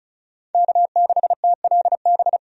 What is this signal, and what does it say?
Morse code, heard as a keyed tone, about 36 wpm, 700 Hz. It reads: K6TLB
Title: USA Call Signs - 5 Characters - 35wpm